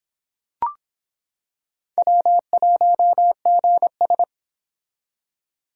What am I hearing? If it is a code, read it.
Morse code, heard as a keyed tone, about 26 wpm, 700 Hz. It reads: W1GS